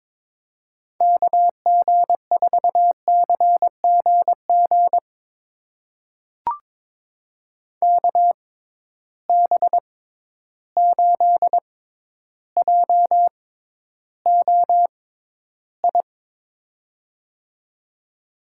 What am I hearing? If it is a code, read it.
Morse code, heard as a keyed tone, about 22 wpm, spaced as for 11 wpm, 700 Hz. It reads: KG4CGG KB8JOI
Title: USA Call Signs - 6x Character Spacing - 22wpm